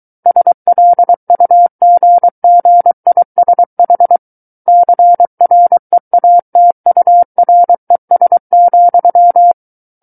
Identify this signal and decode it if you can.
Morse code, heard as a keyed tone, about 23 wpm, 700 Hz. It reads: SLUGGISH CREATURES,
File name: Book One / 14-Chapter 14 - 23 WPM